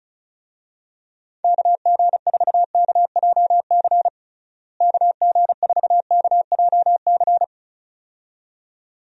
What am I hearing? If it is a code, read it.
Morse code, heard as a keyed tone, about 34 wpm, 700 Hz. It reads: KG4KJC KG4KJC
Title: USA Call Signs - 6 Characters - Repeated 2x - 35wpm